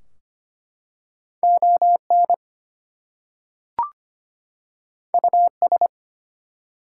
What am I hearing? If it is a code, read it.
Morse code, heard as a keyed tone, about 25 wpm, 700 Hz. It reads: ON US
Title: Two Letter Words - 25wpm